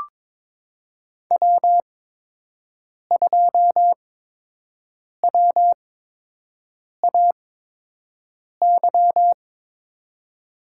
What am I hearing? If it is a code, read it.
Morse code, heard as a keyed tone, about 23 wpm, spaced as for 6 wpm, 700 Hz. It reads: W2WAY
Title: USA Call Signs - 5 Characters - 8x Character Spacing - 22wpm